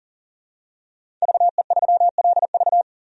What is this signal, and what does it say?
Morse code, heard as a keyed tone, about 40 wpm, 700 Hz. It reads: VE3LV
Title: DXCC-100 Call Signs - 5 Characters - 40wpm